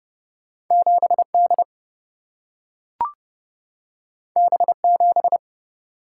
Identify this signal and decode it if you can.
Morse code, heard as a keyed tone, about 30 wpm, 700 Hz. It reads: 7 D B 7
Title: Single-Letter Single-Number 30wpm